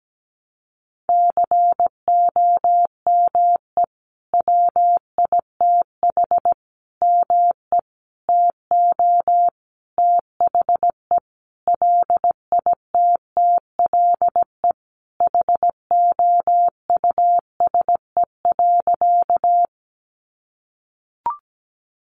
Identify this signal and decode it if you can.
Morse code, heard as a keyed tone, about 17 wpm, 700 Hz. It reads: COME WITH ME TO THE LITTLE HOUSE.
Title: Sentences from Top 200 Words 17wpm